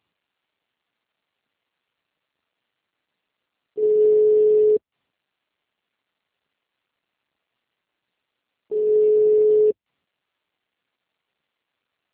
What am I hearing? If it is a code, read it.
Ringback tone (Europe, 425 Hz).